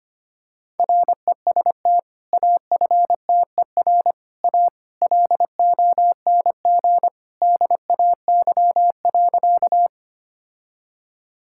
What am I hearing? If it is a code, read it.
Morse code, heard as a keyed tone, about 25 wpm, 700 Hz. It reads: REST AFTER A LONG DAY.
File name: practice / 25wpm / Sentences with 2-Word Phrases - 25wpm